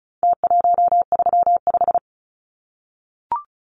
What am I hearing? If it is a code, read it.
Morse code, heard as a keyed tone, about 34 wpm, 700 Hz. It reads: T135
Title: Numbers - Contest Style - 4-Digit 35wpm